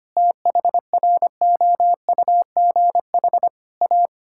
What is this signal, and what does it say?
Morse code, heard as a keyed tone, about 25 wpm, 700 Hz. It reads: THROUGH A
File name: practice / 25wpm / All-Age Appropriate Jokes - 25wpm